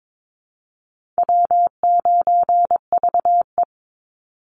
Morse code, heard as a keyed tone, about 22 wpm, 700 Hz. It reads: W9VE
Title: Call Signs Repeated 2x - 22wpm